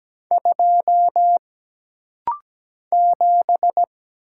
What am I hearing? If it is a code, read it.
Morse code, heard as a keyed tone, about 17 wpm, 700 Hz. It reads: 2 7
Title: Single Letter-Number - Rapid-Fire 17wpm